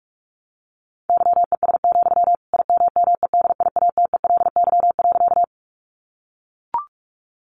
Morse code, heard as a keyed tone, about 44 wpm, 700 Hz. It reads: YES, IMMEDIATELY.
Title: Sentences from Top 1100 Words 45wpm